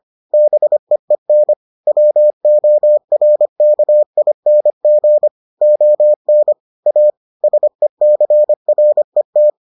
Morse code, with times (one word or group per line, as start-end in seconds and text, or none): 0.3-1.5 BEEN
1.9-5.3 WORKING
5.6-6.5 ON
6.9-7.1 A
7.4-9.5 SECRET